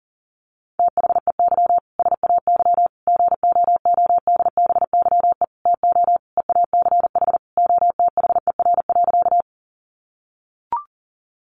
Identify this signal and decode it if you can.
Morse code, heard as a keyed tone, about 40 wpm, 700 Hz. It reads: THEY SAY GOODBYE TO EACH OTHER.
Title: Additional Easy Phrases 40wpm